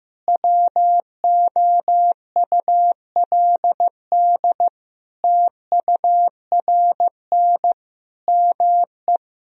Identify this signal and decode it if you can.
Morse code, heard as a keyed tone, about 15 wpm, 700 Hz. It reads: WOULD TURN ME
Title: Sentences from Top 1000 Words - Review Part 4 - 15wpm